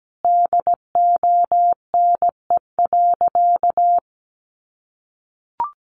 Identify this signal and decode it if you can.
Morse code, heard as a keyed tone, about 17 wpm, 700 Hz. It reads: DONE.
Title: Sentences from Top 700 Words 17wpm